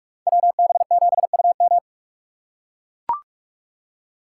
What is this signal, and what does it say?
Morse code, heard as a keyed tone, about 44 wpm, 700 Hz. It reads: WB7UM